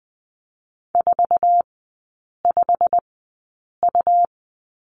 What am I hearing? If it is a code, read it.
Morse code, heard as a keyed tone, about 20 wpm, spaced as for 8 wpm, 700 Hz. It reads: 45U
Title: 100 Lesson 17e - Letter u - Three Characters 20wpm